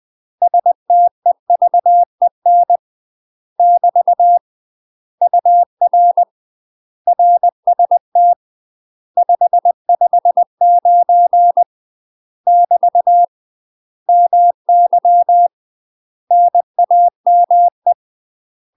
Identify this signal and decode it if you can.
Morse code, heard as a keyed tone, about 20 wpm, 700 Hz. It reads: STEVEN <BT> UR RST 559 <BT> MY NAME
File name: UK MASTER QSOs TRAINING / 20WPM / UK MASTER QSOs 0059